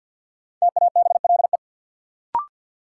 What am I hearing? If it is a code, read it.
Morse code, heard as a keyed tone, about 53 wpm, 700 Hz. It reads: TABLE
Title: Sentences from Top 400 Words - New Word Drill - 50wpm